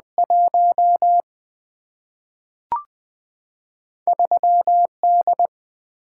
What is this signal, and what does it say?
Morse code, heard as a keyed tone, about 20 wpm, 700 Hz. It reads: 1 3 D
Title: Single-Letter Single-Number 20wpm